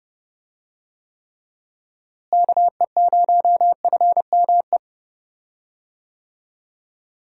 Morse code, heard as a keyed tone, about 31 wpm, 700 Hz. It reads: KE0FME
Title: USA Call Signs - 6 Characters - Repeated 3x - 30wpm